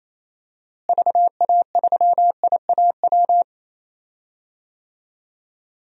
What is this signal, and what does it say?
Morse code, heard as a keyed tone, about 28 wpm, 700 Hz. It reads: VA3IAW
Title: DXCC-100 Call Signs - 6 Characters - Repeated 3x - 28wpm